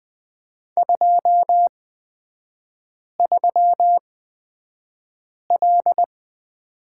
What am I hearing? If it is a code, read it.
Morse code, heard as a keyed tone, about 20 wpm, spaced as for 5 wpm, 700 Hz. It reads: 23L
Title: 250 Lesson 40e - BK - Three Characters 20wpm